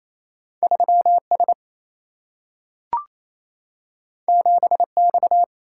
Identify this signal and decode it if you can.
Morse code, heard as a keyed tone, about 28 wpm, 700 Hz. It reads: 3 S 7 X
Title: Single-Letter Single-Number 28wpm